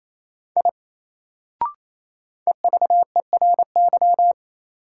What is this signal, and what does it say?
Morse code, heard as a keyed tone, about 28 wpm, 700 Hz. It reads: I EVERY
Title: Sentences from Top 200 Words - New Word Drill - 28wpm